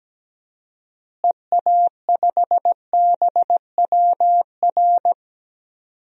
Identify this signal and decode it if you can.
Morse code, heard as a keyed tone, about 17 wpm, 700 Hz. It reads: EA5BWR
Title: DXCC-100 Call Signs - 6 Characters - Repeated 3x - 17wpm